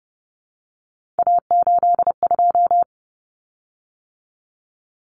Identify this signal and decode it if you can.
Morse code, heard as a keyed tone, about 30 wpm, 700 Hz. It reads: A82
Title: Common Cut Numbers - 3-Digit 30wpm